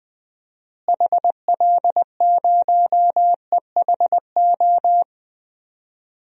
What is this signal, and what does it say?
Morse code, heard as a keyed tone, about 20 wpm, 700 Hz. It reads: HL0EHO
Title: DXCC-100 Call Signs - 6 Characters - Repeated 3x - 20wpm